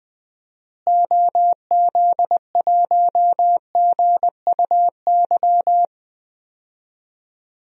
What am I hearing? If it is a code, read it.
Morse code, heard as a keyed tone, about 20 wpm, 700 Hz. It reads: OZ1GUY
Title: DXCC-100 Call Signs - 6 Characters - Repeated 3x - 20wpm